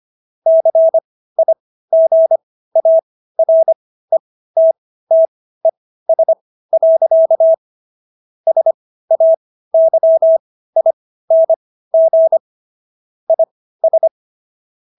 Morse code, heard as a keyed tone, about 25 wpm, spaced as for 15 wpm, 650 Hz. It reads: CIGARETTES. SAYING IS